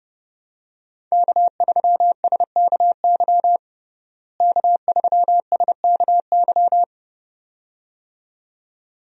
Morse code, heard as a keyed tone, about 32 wpm, 700 Hz. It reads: K3SKY K3SKY
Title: USA Call Signs - 5 Characters - Repeated 2x - 30wpm